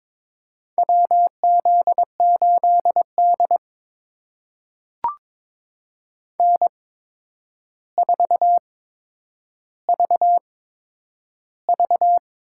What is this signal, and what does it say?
Morse code, heard as a keyed tone, about 22 wpm, spaced as for 9 wpm, 700 Hz. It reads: WZ8D N4VV
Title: USA Call Signs - 4 Characters - 8x Character Spacing - 22wpm